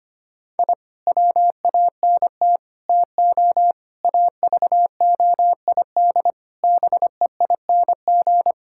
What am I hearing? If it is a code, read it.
Morse code, heard as a keyed tone, about 25 wpm, 700 Hz. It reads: I WANT TO AVOID BEING
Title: Sentences from Top 1000 Words - Review Part 6 - 25wpm